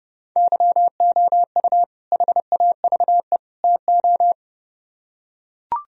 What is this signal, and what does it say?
Morse code, heard as a keyed tone, about 30 wpm, 700 Hz. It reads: YOU HAVE TO